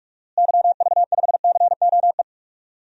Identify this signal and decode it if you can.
Morse code, heard as a keyed tone, about 46 wpm, 700 Hz. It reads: YV5COE
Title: DXCC-100 Call Signs - 6 Characters - Repeated 3x - 45wpm